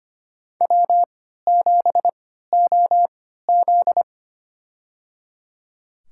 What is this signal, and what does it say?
Morse code, heard as a keyed tone, about 25 wpm, spaced as for 14 wpm, 700 Hz. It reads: W7OZ
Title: USA Call Signs - 4 Characters - 3x Character Spacing - 25wpm